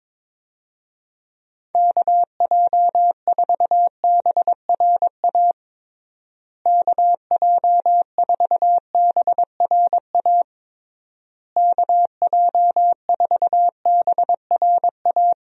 Morse code, heard as a keyed tone, about 22 wpm, 700 Hz. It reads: KJ4BRA KJ4BRA KJ4BRA
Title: USA Call Signs - 6 Characters - Repeated 3x - 22wpm